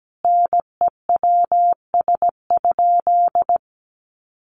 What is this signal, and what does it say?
Morse code, heard as a keyed tone, about 17 wpm, 700 Hz. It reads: NEWS?
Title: Sentences from Top 800 Words 17wpm